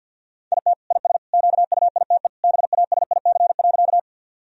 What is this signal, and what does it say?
Morse code, heard as a keyed tone, about 50 wpm, 700 Hz. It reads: IT IS QUITE BASIC.
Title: Sentences from Top 1000 Words - Review Part 3 - 50wpm